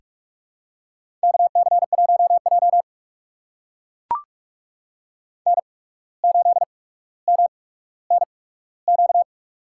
Morse code, heard as a keyed tone, about 42 wpm, spaced as for 17 wpm, 700 Hz. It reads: KC1J N8MNQ
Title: USA Call Signs - 8x Character Spacing - 45wpm